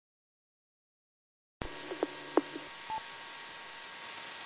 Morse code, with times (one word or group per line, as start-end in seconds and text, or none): none